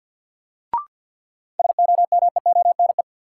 Morse code, heard as a keyed tone, about 46 wpm, 700 Hz. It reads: SOMEONE
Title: Sentences from Top 600 Words - New Word Drill - 50wpm